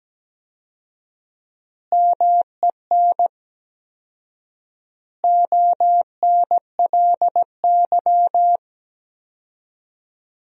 Morse code, heard as a keyed tone, about 17 wpm, 700 Hz. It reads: MEN ONLY
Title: 4x Word-Spacing - Sentences from Top 500 Words - 17wpm